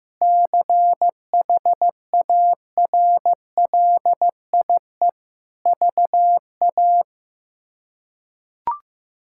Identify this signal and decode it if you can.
Morse code, heard as a keyed tone, about 15 wpm, 700 Hz. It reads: CHARLIE VA
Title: K1USN SST QSO Practice - 15wpm